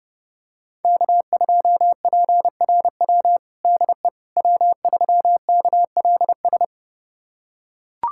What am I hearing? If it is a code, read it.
Morse code, heard as a keyed tone, about 30 wpm, 700 Hz. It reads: K2PRW DE W3KLS
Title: USA Callsign DE Callsign - 5 Characters - 30wpm